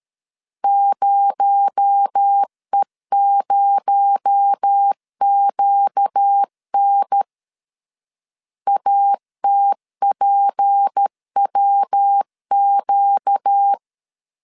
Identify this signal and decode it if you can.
Morse code, heard as a keyed tone, about 13 wpm, 800 Hz. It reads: 0E0QN ATPWQ
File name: LEZIONI / 6 PKX0 / 6 PKX0 12wpm 2231